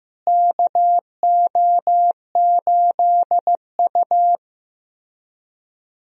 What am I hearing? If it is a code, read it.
Morse code, heard as a keyed tone, about 16 wpm, 700 Hz. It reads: KO8U
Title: USA Call Signs - 4 Characters - Repeated 3x - 15wpm